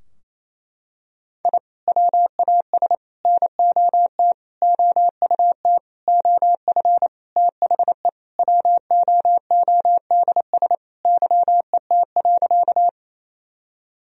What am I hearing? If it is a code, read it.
Morse code, heard as a keyed tone, about 28 wpm, 700 Hz. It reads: I WAS NOT OUT OF THE WOODS YET.
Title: Sentences from Top 1000 Words - Review Part 2 - 28wpm